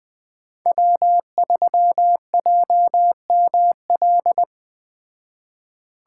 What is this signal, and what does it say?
Morse code, heard as a keyed tone, about 21 wpm, 700 Hz. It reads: W3JML